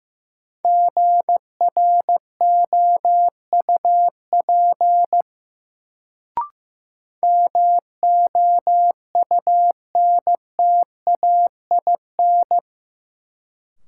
Morse code, heard as a keyed tone, about 15 wpm, 700 Hz. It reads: GROUP MOUNTAIN
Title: Sentences from Top 1000 Words - Review Part 1 - New Word Drill - 15wpm